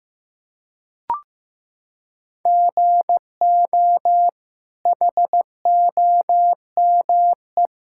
Morse code, heard as a keyed tone, about 15 wpm, 700 Hz. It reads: GO HOME
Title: Sentences from Top 1600 Words 15wpm